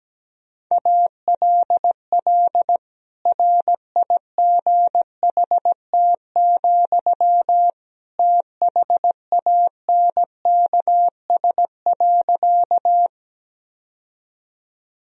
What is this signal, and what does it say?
Morse code, heard as a keyed tone, about 17 wpm, 700 Hz. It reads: ALL RIGHT, THANKS.
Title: Sentences from Top 1000 Words - Review Part 3 - 17wpm